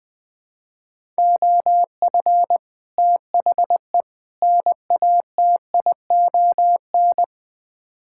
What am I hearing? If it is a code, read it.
Morse code, heard as a keyed tone, about 20 wpm, 700 Hz. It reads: OF THE NATION
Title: Build A Phrase - 3-Word Phrases - 20wpm